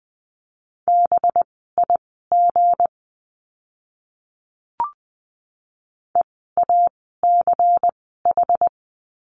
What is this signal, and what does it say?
Morse code, heard as a keyed tone, about 20 wpm, spaced as for 14 wpm, 700 Hz. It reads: BIG EACH
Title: 192 Lesson 31f - Letter v - Words 20wpm